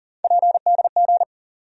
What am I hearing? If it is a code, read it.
Morse code, heard as a keyed tone, about 40 wpm, 700 Hz. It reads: PDG
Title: USA Call Sign Postfixes - 3 Letters 40wpm